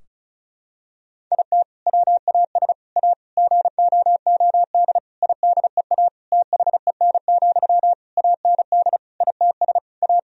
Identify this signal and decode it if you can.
Morse code, heard as a keyed tone, about 36 wpm, 700 Hz. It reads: IT WAS A GOOD IDEA THEN, AND ITS A
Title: Sentences from Top 1000 Words - Review Part 4 - 35wpm